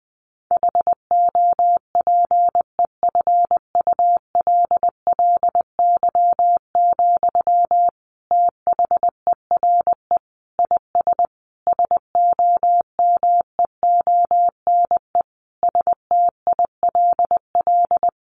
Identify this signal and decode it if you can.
Morse code, heard as a keyed tone, about 20 wpm, 700 Hz. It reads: HOPEFULLY, THERE IS SOMEONE STILL
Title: Sentences from Top 1500 Words 20wpm